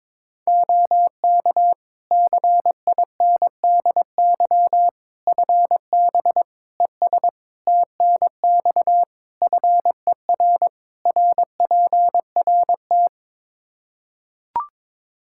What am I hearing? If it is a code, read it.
Morse code, heard as a keyed tone, about 22 wpm, 700 Hz. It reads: OK CINDY FB ES TNX FER RPRT
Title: Practice QSO - Exchange 2 - Info - 22wpm